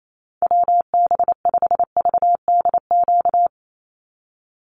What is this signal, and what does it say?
Morse code, heard as a keyed tone, about 28 wpm, 700 Hz. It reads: WB5VDQ
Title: USA Call Signs 28wpm